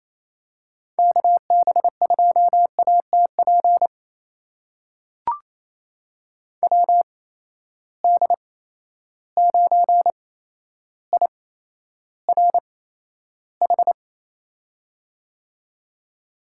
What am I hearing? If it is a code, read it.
Morse code, heard as a keyed tone, about 28 wpm, spaced as for 12 wpm, 700 Hz. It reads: KB2ATP WD9IRH
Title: USA Call Signs - 6 Characters - 8x Character Spacing - 28wpm